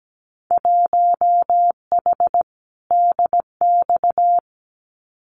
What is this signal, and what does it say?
Morse code, heard as a keyed tone, about 17 wpm, 700 Hz. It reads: 1H DX
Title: Winter Field Day Exchanges 17wpm